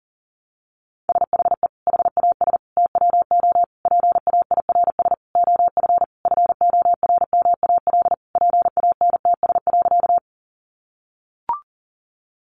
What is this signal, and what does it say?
Morse code, heard as a keyed tone, about 40 wpm, 700 Hz. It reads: SHE HAS TWO PAIRS OF FORMAL PANTS.
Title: Additional Easy Phrases 40wpm